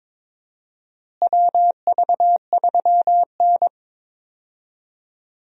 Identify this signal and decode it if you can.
Morse code, heard as a keyed tone, about 22 wpm, 700 Hz. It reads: WV3N
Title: USA Call Signs - 4 Characters - Repeated 3x - 22wpm